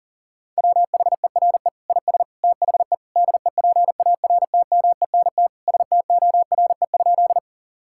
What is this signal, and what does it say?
Morse code, heard as a keyed tone, about 40 wpm, 700 Hz. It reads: WHERE IS THE DEPARTMENT STORE?
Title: Sentences from Top 1000 Words - Review Part 5 - 40wpm